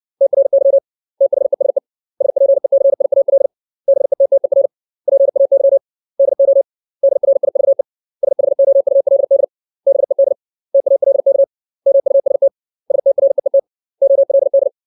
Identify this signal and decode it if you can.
Morse code, heard as a keyed tone, about 61 wpm, 550 Hz. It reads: TRY THESE SPECIAL BETTER PAY DO DRIVE SHOULD BED TALK MUST STREET OLD
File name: 632 words / 632 Words 60WPM